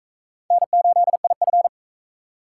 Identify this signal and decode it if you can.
Morse code, heard as a keyed tone, about 38 wpm, 700 Hz. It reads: N8IF